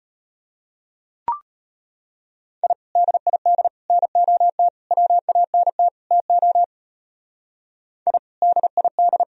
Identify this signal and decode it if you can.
Morse code, heard as a keyed tone, about 33 wpm, 700 Hz. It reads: I DID NOT WANT TO I DID